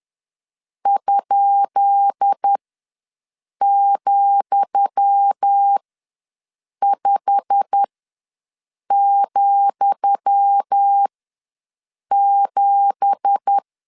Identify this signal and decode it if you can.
Morse code, heard as a keyed tone, about 10 wpm, spaced as for 6 wpm, 800 Hz. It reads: ?,5,7
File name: LEZIONI / 10 SIGNS / 10D COMA ws 10wpm 4314